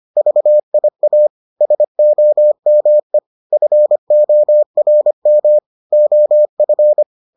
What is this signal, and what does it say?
Morse code, heard as a keyed tone, about 25 wpm, 600 Hz. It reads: VIA SOME FORM OF